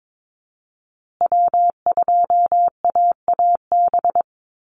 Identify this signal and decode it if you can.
Morse code, heard as a keyed tone, about 22 wpm, 700 Hz. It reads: W2AAB
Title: Call Signs Repeated 2x - 22wpm